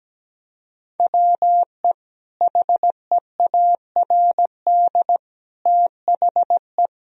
Morse code, heard as a keyed tone, about 17 wpm, 700 Hz. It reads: WE HEARD THE
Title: Sentences from Top 1000 Words - Review Part 2 - 17wpm